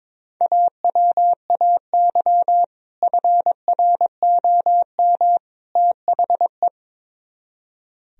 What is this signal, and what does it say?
Morse code, heard as a keyed tone, about 23 wpm, 700 Hz. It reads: AWAY FROM THE
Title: Speed-Racing - Sets of 3 Words 22wpm